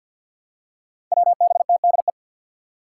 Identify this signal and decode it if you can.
Morse code, heard as a keyed tone, about 51 wpm, 700 Hz. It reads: WBTDE
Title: Five Challenging Letter-Number Combinations 50wpm